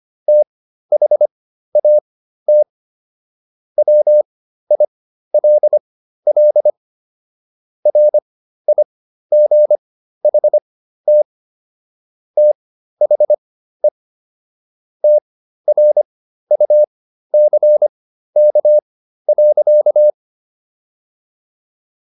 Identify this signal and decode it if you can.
Morse code, heard as a keyed tone, about 25 wpm, spaced as for 13 wpm, 600 Hz. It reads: THAT WILL RIGHT THE TRUCK.